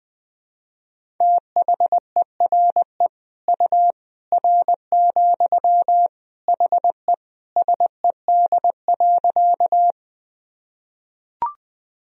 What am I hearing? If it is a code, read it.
Morse code, heard as a keyed tone, about 21 wpm, 700 Hz. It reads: THERE U R, HE SED.
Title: Sentences from Top 100 Words and Abbreviations 20wpm